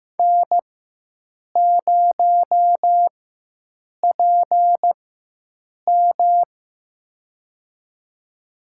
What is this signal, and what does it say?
Morse code, heard as a keyed tone, about 15 wpm, spaced as for 7 wpm, 700 Hz. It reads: N0PM
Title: USA Call Signs - 4 Characters - 4x Character Spacing - 15wpm